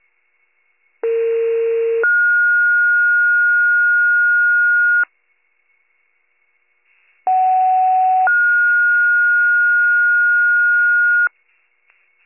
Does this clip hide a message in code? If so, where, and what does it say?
Yes: two-tone page from 1.0 to 11.3 s, heard as a one-second tone, then a three-second tone, about 460 then 1430 Hz, about 750 then 1430 Hz.